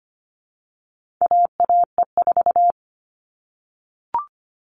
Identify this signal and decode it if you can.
Morse code, heard as a keyed tone, about 25 wpm, 700 Hz. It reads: AAE4
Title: All Cut Numbers - 4-Digit 25wpm